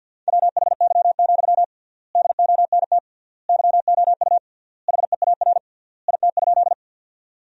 Morse code, heard as a keyed tone, about 46 wpm, 700 Hz. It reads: WHY, DONT YOU HEAR IT?
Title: 3x Word-Spacing - Sentences from Top 300 Words - 50wpm